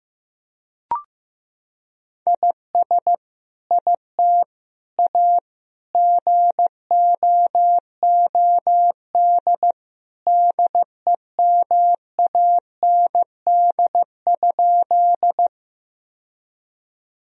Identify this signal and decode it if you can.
Morse code, heard as a keyed tone, about 15 wpm, 700 Hz. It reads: IS IT A GOOD DEMAND?
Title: Sentences from Top 1000 Words - Review Part 4 - 15wpm